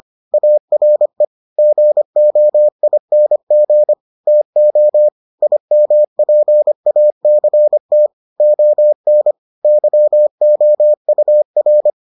Morse code, heard as a keyed tone, about 25 wpm, 600 Hz. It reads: ARE GOING TO IMPACT ON YOUR